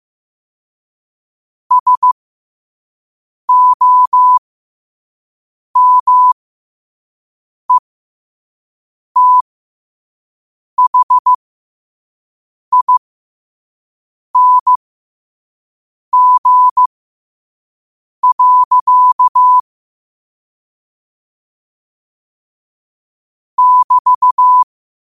Morse code, heard as a keyed tone, about 15 wpm, spaced as for 5 wpm, 1000 Hz. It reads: SOMETHING. <BT>
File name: qotdfiles / 5wpm